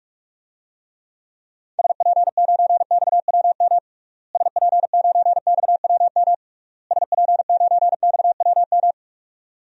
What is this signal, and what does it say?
Morse code, heard as a keyed tone, about 41 wpm, 700 Hz. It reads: SP9XWM SP9XWM SP9XWM
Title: DXCC-100 Call Signs - 6 Characters - Repeated 3x - 45wpm